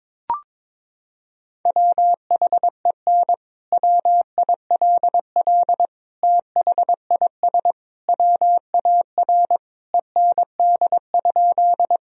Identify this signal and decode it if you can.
Morse code, heard as a keyed tone, about 22 wpm, 700 Hz. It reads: WHEN WILL THIS WAR END?
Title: Sentences from Top 1000 Words - Review Part 2 - 22wpm